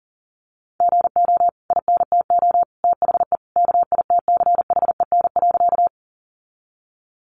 Morse code, heard as a keyed tone, about 40 wpm, 700 Hz. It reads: GO INTO THE KITCHEN.
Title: Sentences from Top 1400 Words 40wpm